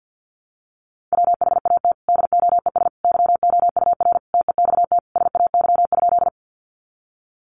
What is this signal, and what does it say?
Morse code, heard as a keyed tone, about 48 wpm, 700 Hz. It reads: WHAT DOES YOUR TEXT SAY?
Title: Sentences from Top 1200 Words 50wpm